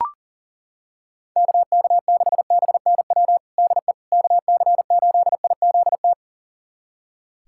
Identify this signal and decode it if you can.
Morse code, heard as a keyed tone, about 40 wpm, 700 Hz. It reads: KK6BNW DE KC8IZT
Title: USA Callsign DE Callsign 40wpm